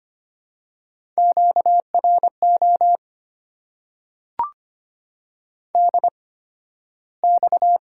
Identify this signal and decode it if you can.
Morse code, heard as a keyed tone, about 25 wpm, spaced as for 13 wpm, 700 Hz. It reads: QRO DX
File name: Top 90 QSO Elements - 8x Character Spacing - 25wpm